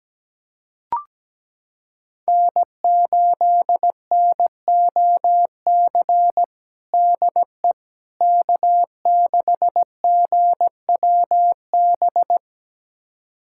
Morse code, heard as a keyed tone, about 17 wpm, 700 Hz. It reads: N8NOC DE K6GWB
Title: USA Callsign DE Callsign - 5 Characters - 17wpm